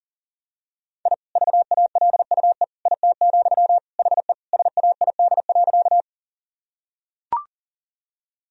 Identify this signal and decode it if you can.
Morse code, heard as a keyed tone, about 40 wpm, 700 Hz. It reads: I VALUE IT, HE SAID.
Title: Sentences from Top 1000 Words - Review Part 3 - 40wpm